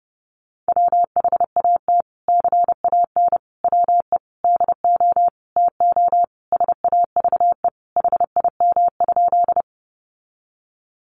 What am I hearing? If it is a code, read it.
Morse code, heard as a keyed tone, about 30 wpm, 700 Hz. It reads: WHAT CAN WE DO TO SAVE HIM?
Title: Sentences from Top 1100 Words 30wpm